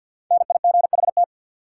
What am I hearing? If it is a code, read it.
Morse code, heard as a keyed tone, about 53 wpm, 700 Hz. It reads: NIGHT
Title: Sentences from Top 300 Words - New Word Drill - 50wpm